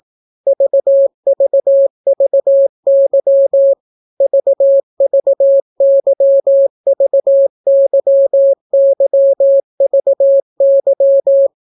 Morse code, heard as a keyed tone, about 18 wpm, 550 Hz. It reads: VVVY VVYVYYVY